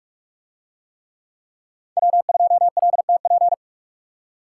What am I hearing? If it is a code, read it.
Morse code, heard as a keyed tone, about 44 wpm, 700 Hz. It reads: W2LTP